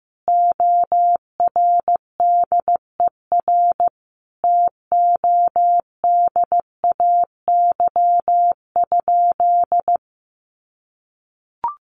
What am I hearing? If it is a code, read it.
Morse code, heard as a keyed tone, about 15 wpm, 700 Hz. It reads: ORDER TODAY?
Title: Sentences from Top 1500 Words 15wpm